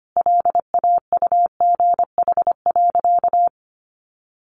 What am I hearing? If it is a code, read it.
Morse code, heard as a keyed tone, about 25 wpm, 700 Hz. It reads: LAUGH.
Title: Sentences from Top 1500 Words 25wpm